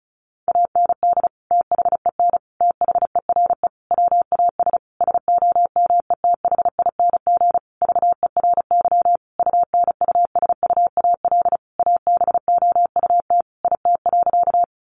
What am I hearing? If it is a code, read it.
Morse code, heard as a keyed tone, about 34 wpm, 700 Hz. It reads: AND THEN THERE WAS SOMETHING VERY UNUSUAL ABOUT IT.
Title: Sentences from Top 1400 Words 35wpm